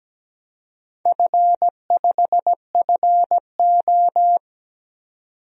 Morse code, heard as a keyed tone, about 17 wpm, 700 Hz. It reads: F5FO
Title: DXCC-100 Call Signs - 4 Characters - 17wpm